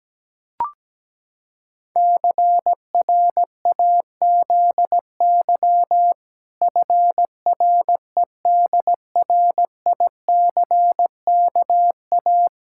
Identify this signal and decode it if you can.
Morse code, heard as a keyed tone, about 17 wpm, 700 Hz. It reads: CRAZY FREDRICKA
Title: Pangram Challenge v2 - 17wpm